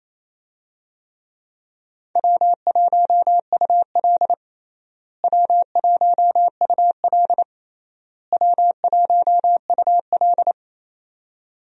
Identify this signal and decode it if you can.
Morse code, heard as a keyed tone, about 28 wpm, 700 Hz. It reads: W1UL W1UL W1UL
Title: USA Call Signs - 4 Characters - Repeated 3x - 28wpm